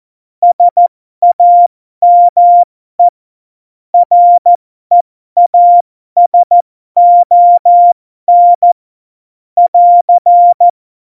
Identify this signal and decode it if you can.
Morse code, heard as a keyed tone, about 14 wpm, spaced as for 12 wpm, 700 Hz. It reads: SAME REASON <AR>
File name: Morsekurs / Witze